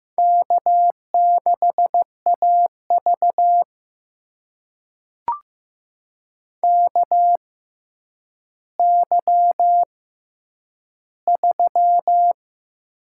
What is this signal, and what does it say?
Morse code, heard as a keyed tone, about 15 wpm, spaced as for 9 wpm, 700 Hz. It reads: K6AV KY3
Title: USA Call Signs - 4 Characters - 6x Character Spacing - 15wpm